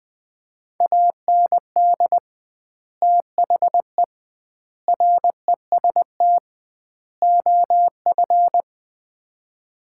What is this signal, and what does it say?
Morse code, heard as a keyed tone, about 19 wpm, 700 Hz. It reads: AND THE REST OF
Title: 2x Word-Spacing - Sets of 4 Words - 20wpm